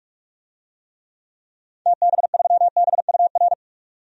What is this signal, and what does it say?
Morse code, heard as a keyed tone, about 46 wpm, 700 Hz. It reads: TB3BUR
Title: DXCC-100 Call Signs - 6 Characters - 45wpm